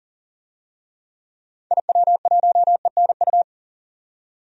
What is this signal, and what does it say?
Morse code, heard as a keyed tone, about 40 wpm, 700 Hz. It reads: IW1ENU
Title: DXCC-100 Call Signs - 6 Characters - 40wpm